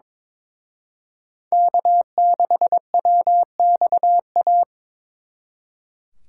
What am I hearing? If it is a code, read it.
Morse code, heard as a keyed tone, about 25 wpm, 700 Hz. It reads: K6WXA
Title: Speed Racing - USA 1x3 Call Signs 22wpm